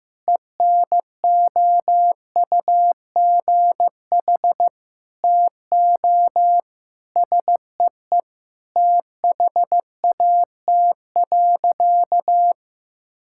Morse code, heard as a keyed tone, about 15 wpm, 700 Hz. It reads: ENOUGH TO SEE THAT.
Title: Sentences from Top 1600 Words 15wpm